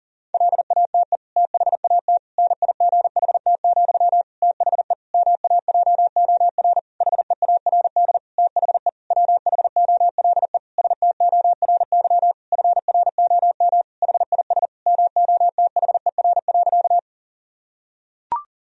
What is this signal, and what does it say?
Morse code, heard as a keyed tone, about 40 wpm, 700 Hz. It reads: LATE THAT NIGHT, THE MAJOR HEARD THE WHOLE STORY FROM HIS MOTHER.
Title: Sentences from Top 1000 Words - Review Part 4 - 40wpm